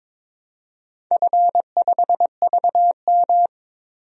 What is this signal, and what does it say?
Morse code, heard as a keyed tone, about 22 wpm, 700 Hz. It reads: F5VM
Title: DXCC-100 Call Signs - 4 Characters - Repeated 3x - 22wpm